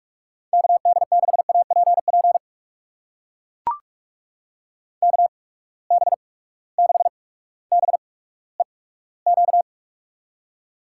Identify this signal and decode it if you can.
Morse code, heard as a keyed tone, about 42 wpm, spaced as for 19 wpm, 700 Hz. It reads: KD6APP KB6BEQ
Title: USA Call Signs - 6 Characters - 8x Character Spacing - 45wpm